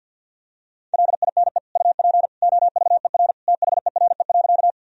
Morse code, heard as a keyed tone, about 52 wpm, 700 Hz. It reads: LINE UP OVER THERE.
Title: Sentences from Top 1000 Words - Review Part 1 - 50wpm